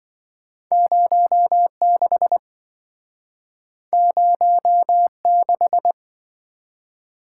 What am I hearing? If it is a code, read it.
Morse code, heard as a keyed tone, about 27 wpm, 700 Hz. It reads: 06 06